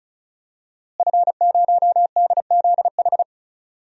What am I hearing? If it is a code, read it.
Morse code, heard as a keyed tone, about 34 wpm, 700 Hz. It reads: F0DZH
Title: DXCC-100 Call Signs - 5 Characters - Repeated 2x - 35wpm